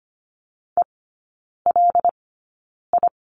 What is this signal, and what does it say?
Morse code, heard as a keyed tone, about 25 wpm, spaced as for 9 wpm, 700 Hz. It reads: ELI
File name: Lesson 17e - Letter u - Three Characters 25wpm